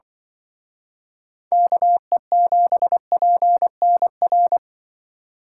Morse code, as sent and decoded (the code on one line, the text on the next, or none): -.- . --... .--. -. .-.
KE7PNR